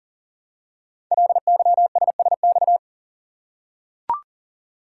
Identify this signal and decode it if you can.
Morse code, heard as a keyed tone, about 42 wpm, 700 Hz. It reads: LYSSX
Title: Code Groups - 5 Letters - 40wpm